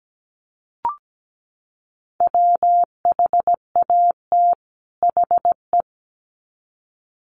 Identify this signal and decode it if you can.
Morse code, heard as a keyed tone, about 17 wpm, 700 Hz. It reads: WHAT HE
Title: Sets of 2 Words 17wpm